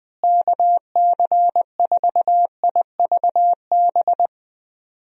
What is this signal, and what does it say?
Morse code, heard as a keyed tone, about 21 wpm, 700 Hz. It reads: KC4IVB
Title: USA Call Signs - 6 Characters - Repeated 3x - 20wpm